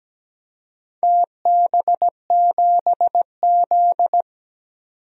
Morse code, heard as a keyed tone, about 17 wpm, 700 Hz. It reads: TB7Z
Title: DXCC-100 Call Signs - 4 Characters - 17wpm